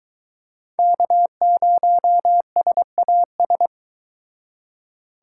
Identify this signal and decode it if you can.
Morse code, heard as a keyed tone, about 23 wpm, 700 Hz. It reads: K0SAS